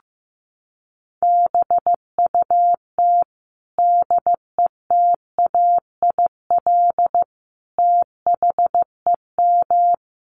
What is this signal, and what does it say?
Morse code, heard as a keyed tone, about 15 wpm, 700 Hz. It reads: BUT DETAIL THEM
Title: Sentences from Top 1500 Words 15wpm